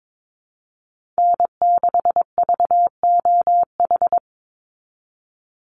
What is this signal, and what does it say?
Morse code, heard as a keyed tone, about 22 wpm, 700 Hz. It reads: N6VOH
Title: Call Signs Repeated 3x - 22wpm